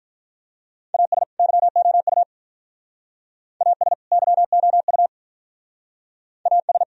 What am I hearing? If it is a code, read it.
Morse code, heard as a keyed tone, about 46 wpm, 700 Hz. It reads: AS YOU AS YOU AS